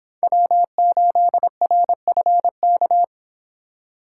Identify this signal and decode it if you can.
Morse code, heard as a keyed tone, about 26 wpm, 700 Hz. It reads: W8RFK